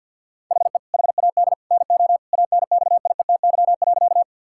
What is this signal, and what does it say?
Morse code, heard as a keyed tone, about 50 wpm, 700 Hz. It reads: HE HAD NO ANXIETY.
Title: Sentences from Top 1600 Words 50wpm